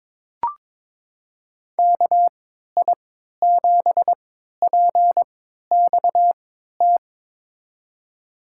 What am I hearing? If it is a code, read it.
Morse code, heard as a keyed tone, about 22 wpm, spaced as for 13 wpm, 700 Hz. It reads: KI7PXT
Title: USA Call Signs - 6 Characters - 3x Character Spacing - 22wpm